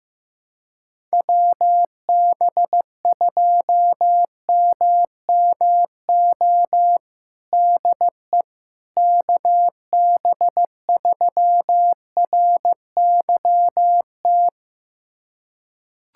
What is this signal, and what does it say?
Morse code, heard as a keyed tone, about 15 wpm, 700 Hz. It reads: WB2MMO DE KB3RYT
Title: USA Callsign DE Callsign - 6 Characters - 15wpm